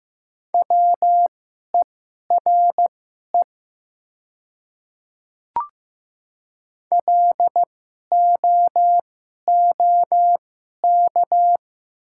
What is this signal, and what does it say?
Morse code, heard as a keyed tone, about 15 wpm, spaced as for 11 wpm, 700 Hz. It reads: WERE LOOK
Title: 2x Character Spacing - Top 100 Words - 15wpm